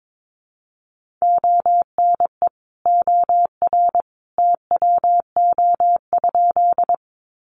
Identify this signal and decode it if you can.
Morse code, heard as a keyed tone, about 22 wpm, 700 Hz. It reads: ONE OR TWO?
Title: Sentences from Top 100 Words 22wpm